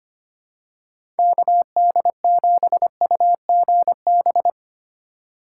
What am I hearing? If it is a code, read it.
Morse code, heard as a keyed tone, about 25 wpm, 700 Hz. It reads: KD7UGB
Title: USA Call Signs - 4x Character Spacing - 25wpm